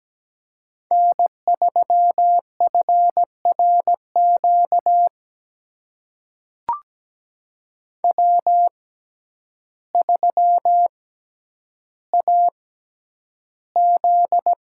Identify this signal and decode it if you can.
Morse code, heard as a keyed tone, about 17 wpm, spaced as for 9 wpm, 700 Hz. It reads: N3FRQ W3AZ
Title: USA Call Signs - 5 Characters - 6x Character Spacing - 17wpm